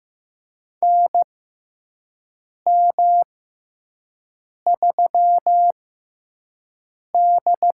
Morse code, heard as a keyed tone, about 15 wpm, spaced as for 5 wpm, 700 Hz. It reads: NM3D
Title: USA Call Signs - 4 Characters - 6x Character Spacing - 15wpm